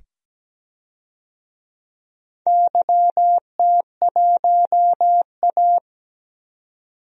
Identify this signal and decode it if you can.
Morse code, heard as a keyed tone, about 17 wpm, 700 Hz. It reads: YT1A
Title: DXCC-100 Call Signs - 4 Characters - 17wpm